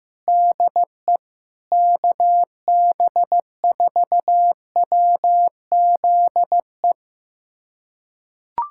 Morse code, heard as a keyed tone, about 15 wpm, 700 Hz. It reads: DE KB4WZE
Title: USA Callsign DE Callsign 15wpm